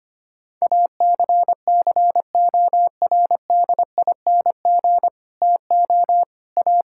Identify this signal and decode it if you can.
Morse code, heard as a keyed tone, about 25 wpm, 700 Hz. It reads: ACCORDING TO A
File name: Sets of 9 Words - 25wpm